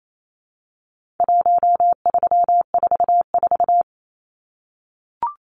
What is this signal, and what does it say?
Morse code, heard as a keyed tone, about 28 wpm, 700 Hz. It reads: 1344
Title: Numbers - 4-Digit 28wpm